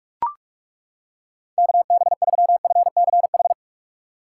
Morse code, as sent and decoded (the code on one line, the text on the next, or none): -.- -... ...-- ..-. -.-. ....
KB3FCH